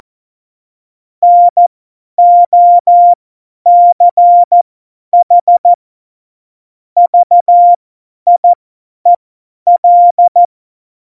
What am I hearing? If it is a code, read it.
Morse code, heard as a keyed tone, about 14 wpm, spaced as for 10 wpm, 700 Hz. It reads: NOCH VIEL